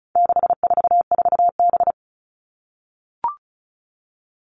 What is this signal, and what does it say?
Morse code, heard as a keyed tone, about 34 wpm, 700 Hz. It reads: 644B
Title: All Cut Numbers - 4-Digit 35wpm